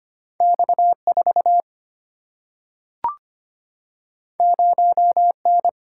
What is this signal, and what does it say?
Morse code, heard as a keyed tone, about 25 wpm, 700 Hz. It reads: X4 0N
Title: Single-Letter Single-Number 25wpm